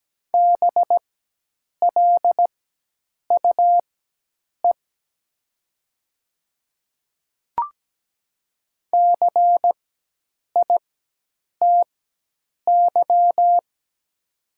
Morse code, heard as a keyed tone, about 17 wpm, spaced as for 8 wpm, 700 Hz. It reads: BLUE CITY
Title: 4x Character Spacing - Top 500 Words - 17wpm